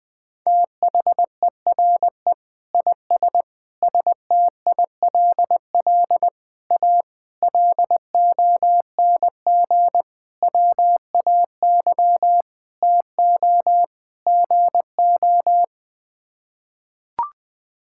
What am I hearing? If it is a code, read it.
Morse code, heard as a keyed tone, about 21 wpm, 700 Hz. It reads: THERE IS STILL A LONG WAY TO GO
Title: Sets of 8 Words - 20wpm